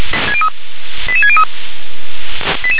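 Five-tone selective call (ZVEI), heard as quick runs of five tones, tones of about 70 ms. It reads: 88603
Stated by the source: ZVEI five-tone selective call 88851 received on SDR.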